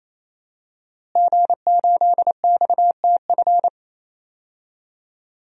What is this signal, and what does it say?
Morse code, heard as a keyed tone, about 27 wpm, 700 Hz. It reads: G8XTF
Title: DXCC-100 Call Signs - 5 Characters - Repeated 3x - 28wpm